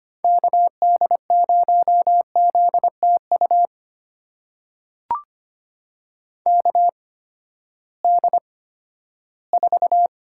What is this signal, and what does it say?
Morse code, heard as a keyed tone, about 25 wpm, spaced as for 14 wpm, 700 Hz. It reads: KD0ZTU KD4
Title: USA Call Signs - 6 Characters - 8x Character Spacing - 25wpm